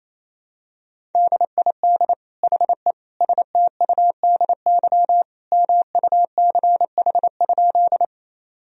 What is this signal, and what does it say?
Morse code, heard as a keyed tone, about 27 wpm, 700 Hz. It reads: DID HE STUDY MUCH?
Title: Sentences from Top 1000 Words - Review Part 1 - 28wpm